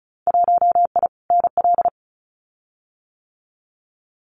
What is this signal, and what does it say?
Morse code, heard as a keyed tone, about 36 wpm, 700 Hz. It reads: 1I NL
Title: Winter Field Day Exchanges 35wpm